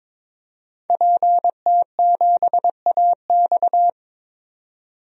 Morse code, heard as a keyed tone, about 22 wpm, 700 Hz. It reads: PT7AX
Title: DXCC-100 Call Signs - 5 Characters - 22wpm